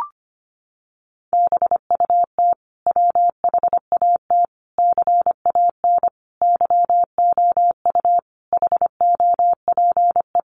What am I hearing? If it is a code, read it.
Morse code, heard as a keyed tone, about 25 wpm, 700 Hz. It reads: BUT WHAT CAN YOU HOPE
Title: Sentences from Top 1000 Words 25wpm